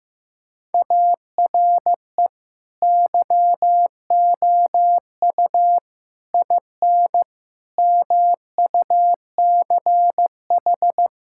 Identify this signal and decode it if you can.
Morse code, heard as a keyed tone, about 16 wpm, 700 Hz. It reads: ARE YOU IN MUCH
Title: Sentences from Top 1600 Words 15wpm